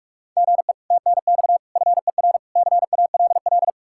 Morse code, heard as a keyed tone, about 56 wpm, 700 Hz. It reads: GE TNX FER CALL
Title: Speed-Racing Common QSO Phrases - 45wpm